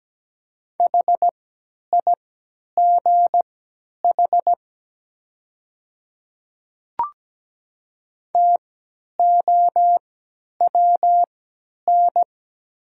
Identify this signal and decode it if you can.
Morse code, heard as a keyed tone, about 17 wpm, spaced as for 10 wpm, 700 Hz. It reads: HIGH TOWN
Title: 3x Character Spacing - Top 500 Words - 17wpm